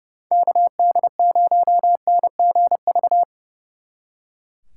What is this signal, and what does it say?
Morse code, heard as a keyed tone, about 36 wpm, 700 Hz. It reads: KD0NGV
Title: Speed Racing - USA 2x3 Call Signs 30wpm